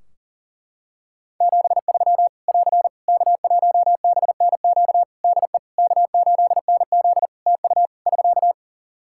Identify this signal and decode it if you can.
Morse code, heard as a keyed tone, about 40 wpm, 700 Hz. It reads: 73 <AR> K1BNQ DE K8NZ TU <SK>
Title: Practice QSO - Exchange 4 - Ending - 40wpm